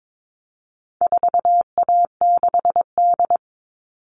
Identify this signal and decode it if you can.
Morse code, heard as a keyed tone, about 22 wpm, 700 Hz. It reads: 4A6D
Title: All Cut Numbers - 4-Digit 22wpm